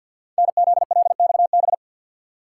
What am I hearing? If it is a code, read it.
Morse code, heard as a keyed tone, about 51 wpm, 690 Hz. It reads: NZLXB
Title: Code Groups - 5 Letters - 50wpm